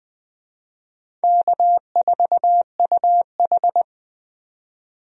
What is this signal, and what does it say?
Morse code, heard as a keyed tone, about 20 wpm, 700 Hz. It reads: K4UH
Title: USA Call Signs - 4 Characters - 20wpm